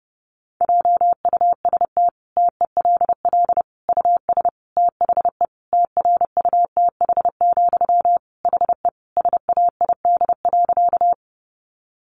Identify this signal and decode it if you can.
Morse code, heard as a keyed tone, about 30 wpm, 700 Hz. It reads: JUST TELL US THE TRUTH, HE SAID.
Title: Sentences from Top 800 Words 30wpm